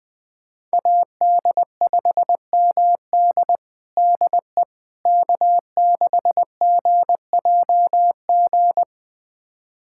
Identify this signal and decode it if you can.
Morse code, heard as a keyed tone, about 20 wpm, 700 Hz. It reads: AD5MD DE K6GJG
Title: USA Callsign DE Callsign - 5 Characters - 20wpm